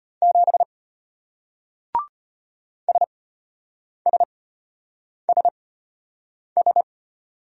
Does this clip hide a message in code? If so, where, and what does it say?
Yes: Morse code from 0.2 to 6.8 s, heard as a keyed tone, about 33 wpm, spaced as for 8 wpm, 700 Hz. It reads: 7 SSSS